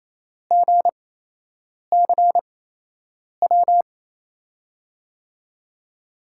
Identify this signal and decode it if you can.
Morse code, heard as a keyed tone, about 28 wpm, spaced as for 8 wpm, 700 Hz. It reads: GCW
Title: USA Call Signs - 8x Character Spacing - 28wpm